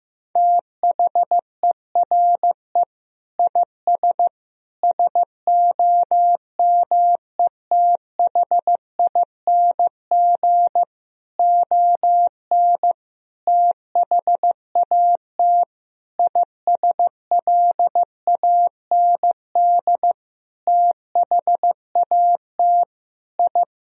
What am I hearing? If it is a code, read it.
Morse code, heard as a keyed tone, about 15 wpm, 700 Hz. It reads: THERE IS SOMETHING ON THAT ISLAND THAT I
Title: Sentences from Top 1000 Words - Review Part 3 - 15wpm